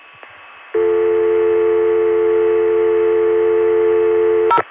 Dial tone (North America), 350 + 440 Hz.